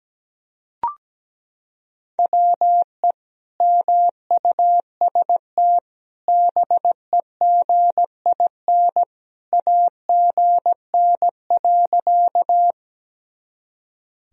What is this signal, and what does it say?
Morse code, heard as a keyed tone, about 17 wpm, 700 Hz. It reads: WE MUST BEGIN AGN.
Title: Sentences from Top 300 Words and Abbreviations 17wpm